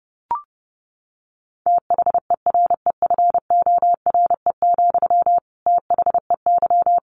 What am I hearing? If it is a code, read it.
Morse code, heard as a keyed tone, about 30 wpm, 700 Hz. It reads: THEREFORE, THEY
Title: Sentences from Top 1200 Words 30wpm